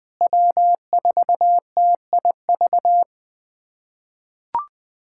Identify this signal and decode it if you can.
Morse code, heard as a keyed tone, about 20 wpm, 700 Hz. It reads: W4TIV
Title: USA Callsign DE Callsign 20wpm